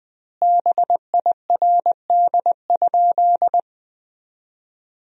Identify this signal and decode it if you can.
Morse code, heard as a keyed tone, about 21 wpm, 700 Hz. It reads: BIRD?
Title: 2x Word-Spacing - Sentences from Top 400 Words - 20wpm